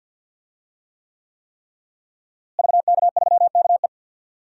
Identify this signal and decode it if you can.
Morse code, heard as a keyed tone, about 52 wpm, 700 Hz. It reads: VK3KE
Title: DXCC-100 Call Signs - 5 Characters - Repeated 3x - 50wpm